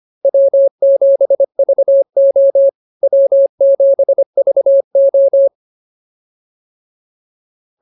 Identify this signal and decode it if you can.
Morse code, heard as a keyed tone, about 25 wpm, 550 Hz. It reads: W7VO W7VO